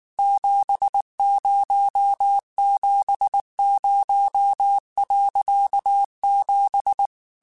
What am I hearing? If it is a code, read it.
Morse code, heard as a keyed tone, about 19 wpm, 800 Hz. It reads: 7070.7
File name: QSOs / QSOs-mp3s / kc7sac@18